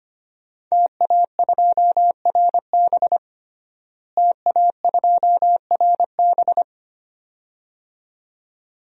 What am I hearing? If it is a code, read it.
Morse code, heard as a keyed tone, about 25 wpm, 700 Hz. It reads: TA2RB TA2RB
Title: DXCC-100 Call Signs - 5 Characters - Repeated 2x - 25wpm